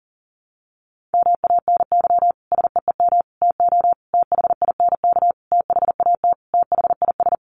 Morse code, heard as a keyed tone, about 40 wpm, 700 Hz. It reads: MANY SEEM TO THINK THAT THIS
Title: Sentences from Top 1400 Words 40wpm